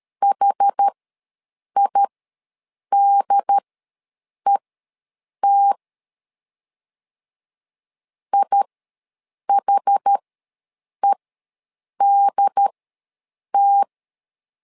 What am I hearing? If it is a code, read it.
Morse code, heard as a keyed tone, about 13 wpm, spaced as for 7 wpm, 800 Hz. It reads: HIDET IHEDT
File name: LEZIONI / 2 UHDN / 2 UHD ws 12wpm 1722